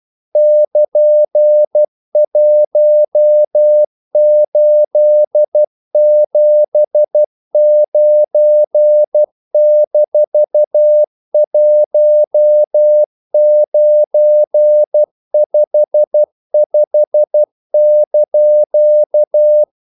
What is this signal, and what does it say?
Morse code, heard as a keyed tone, about 12 wpm, 600 Hz. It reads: <KN>1879-1955)
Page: English quotations in CW (telegraphy - morse)